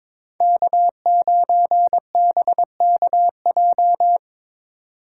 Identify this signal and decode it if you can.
Morse code, heard as a keyed tone, about 25 wpm, 700 Hz. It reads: K9BKJ
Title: Speed Racing - USA 1x3 Call Signs 22wpm